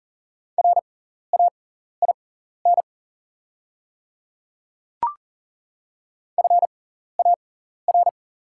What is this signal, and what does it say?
Morse code, heard as a keyed tone, about 40 wpm, spaced as for 14 wpm, 690 Hz. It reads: RAIN FAR
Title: 6x Character Spacing - Top 500 Words - 40wpm